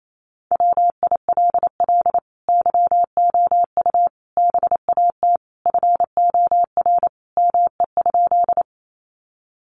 Morse code, heard as a keyed tone, about 28 wpm, 700 Hz. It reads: WILL YOU BAT FOR ME?
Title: Sentences from Top 1500 Words 28wpm